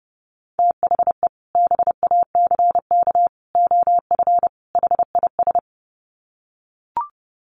Morse code, heard as a keyed tone, about 30 wpm, 700 Hz. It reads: THE BACK OF HIS
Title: Sets of 4 Words 30wpm